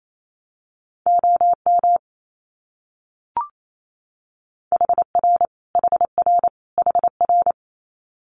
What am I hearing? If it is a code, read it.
Morse code, heard as a keyed tone, about 28 wpm, 700 Hz. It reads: OM HR HR HR
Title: Abbreviations Repeated 3x - 28wpm